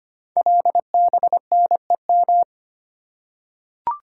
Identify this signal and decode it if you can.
Morse code, heard as a keyed tone, about 25 wpm, 700 Hz. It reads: LBNEM
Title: Code Groups - 5 Letters - 25wpm